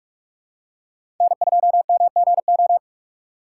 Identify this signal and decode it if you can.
Morse code, heard as a keyed tone, about 43 wpm, 700 Hz. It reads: N2MGO
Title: USA Call Signs - 5 Characters - 3x Character Spacing - 45wpm